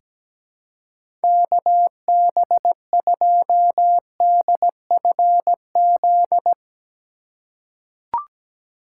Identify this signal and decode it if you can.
Morse code, heard as a keyed tone, about 17 wpm, 700 Hz. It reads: KB2DFZ
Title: USA Call Signs - 6 Characters - 4x Character Spacing - 17wpm